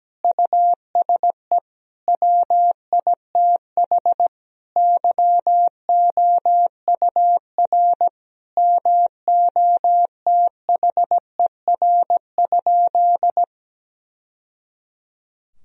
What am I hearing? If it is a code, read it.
Morse code, heard as a keyed tone, about 17 wpm, 700 Hz. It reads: USE WITH YOUR MOTHER?
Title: Sentences from Top 1000 Words - Review Part 3 - 17wpm